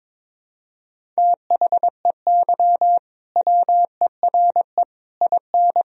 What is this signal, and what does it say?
Morse code, heard as a keyed tone, about 22 wpm, 700 Hz. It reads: THEY WERE IN
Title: Sentences from Top 1000 Words - Review Part 1 - 22wpm